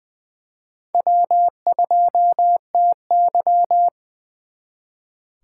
Morse code, heard as a keyed tone, about 24 wpm, 700 Hz. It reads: W2TY
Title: Speed Racing - USA 1x2 Call Signs 20wpm